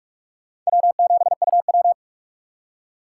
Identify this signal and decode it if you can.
Morse code, heard as a keyed tone, about 45 wpm, 700 Hz. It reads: W7UW
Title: USA Call Signs - 2x Character Spacing - 45wpm